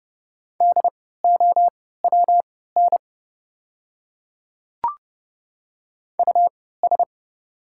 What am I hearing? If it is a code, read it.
Morse code, heard as a keyed tone, about 30 wpm, spaced as for 17 wpm, 700 Hz. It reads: DOWN US
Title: 3x Character Spacing - Top 300 Words - 30wpm